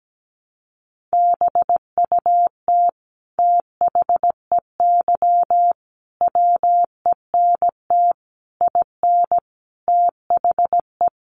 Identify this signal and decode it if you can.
Morse code, heard as a keyed tone, about 17 wpm, 700 Hz. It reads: BUT THEY WENT IN THE
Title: Sentences from Top 1300 Words 17wpm